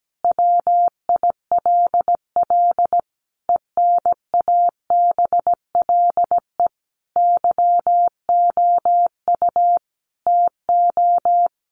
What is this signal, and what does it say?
Morse code, heard as a keyed tone, about 17 wpm, 700 Hz. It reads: WILL ENABLE YOU TO
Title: Sentences from Top 1500 Words 17wpm